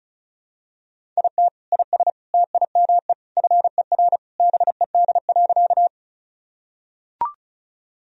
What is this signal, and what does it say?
Morse code, heard as a keyed tone, about 34 wpm, 700 Hz. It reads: IT IS TIME FER BED.
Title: Sentences from Top 500 Words and Abbreviations 35wpm